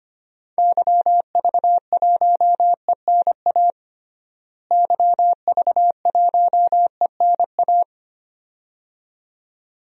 Morse code, heard as a keyed tone, about 25 wpm, 700 Hz. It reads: YV1ENA YV1ENA
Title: DXCC-100 Call Signs - 6 Characters - Repeated 2x - 25wpm